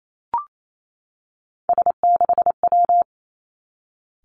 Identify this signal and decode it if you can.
Morse code, heard as a keyed tone, about 28 wpm, 700 Hz. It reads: S6W
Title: Three Challenging Letter-Number Combinations 28wpm